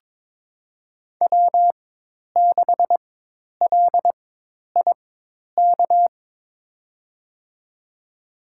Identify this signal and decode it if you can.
Morse code, heard as a keyed tone, about 22 wpm, spaced as for 10 wpm, 700 Hz. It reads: W6LIK
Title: USA Call Signs - 5 Characters - 4x Character Spacing - 22wpm